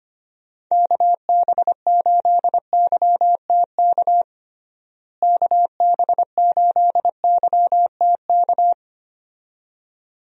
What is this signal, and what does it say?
Morse code, heard as a keyed tone, about 25 wpm, 700 Hz. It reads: KB8YTK KB8YTK
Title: USA Call Signs - 6 Characters - Repeated 2x - 25wpm